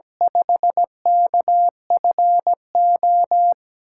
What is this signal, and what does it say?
Morse code, heard as a keyed tone, about 17 wpm, 700 Hz. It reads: 5KFO
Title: USA Call Signs - 6 Characters - 8x Character Spacing - 17wpm